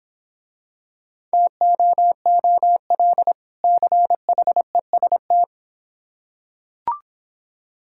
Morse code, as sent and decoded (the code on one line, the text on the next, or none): - --- --- .-.. / -.-. .... ... -
TOOL CHEST